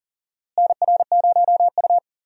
Morse code, heard as a keyed tone, about 40 wpm, 700 Hz. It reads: NR0U
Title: USA Call Signs - 4 Characters - Repeated 3x - 40wpm